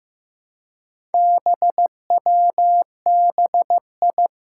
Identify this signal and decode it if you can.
Morse code, heard as a keyed tone, about 15 wpm, 700 Hz. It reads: BWBI
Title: Four Challenging Letter-Number Combinations 15wpm